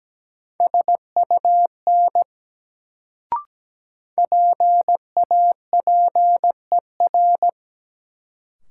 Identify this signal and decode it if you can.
Morse code, heard as a keyed tone, about 17 wpm, 700 Hz. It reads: SUN PAPER
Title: Sentences from Top 300 Words - New Word Drill - 17wpm